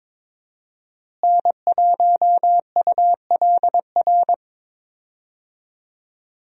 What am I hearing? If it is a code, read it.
Morse code, heard as a keyed tone, about 22 wpm, 700 Hz. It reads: N1ULR
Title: USA Call Signs - 5 Characters - Repeated 2x - 22wpm